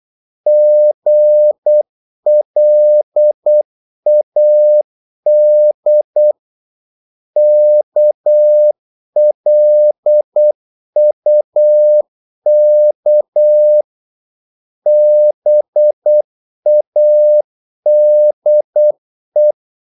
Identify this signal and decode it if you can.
Morse code, heard as a keyed tone, about 8 wpm, 600 Hz. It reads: GLAD KLUK BADE
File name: Korte ord DK 8wpm